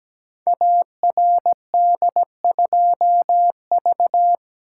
Morse code, heard as a keyed tone, about 17 wpm, 700 Hz. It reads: ARD2V
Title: Code Groups - 5 Letter:Numbers - 17wpm